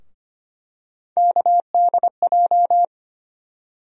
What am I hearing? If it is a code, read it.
Morse code, heard as a keyed tone, about 25 wpm, 700 Hz. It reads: KDJ
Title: USA Call Sign Postfixes - 3 Letters 25wpm